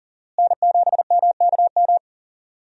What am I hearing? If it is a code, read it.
Morse code, heard as a keyed tone, about 40 wpm, 700 Hz. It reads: N7MKM
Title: USA Call Signs - 5 Characters - Repeated 2x - 40wpm